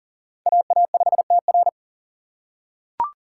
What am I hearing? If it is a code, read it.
Morse code, heard as a keyed tone, about 40 wpm, 700 Hz. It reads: AA5TR
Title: USA Call Signs - 5 Characters - 4x Character Spacing - 40wpm